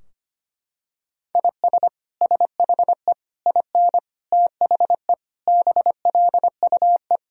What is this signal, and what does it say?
Morse code, heard as a keyed tone, about 25 wpm, 700 Hz. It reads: IS SHE IN THE BLUE
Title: Sentences from Top 1000 Words - Review Part 3 - 25wpm